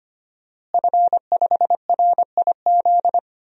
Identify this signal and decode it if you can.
Morse code, heard as a keyed tone, about 25 wpm, 700 Hz. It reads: F5RIZ
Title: DXCC-100 Call Signs - 5 Characters - Repeated 3x - 25wpm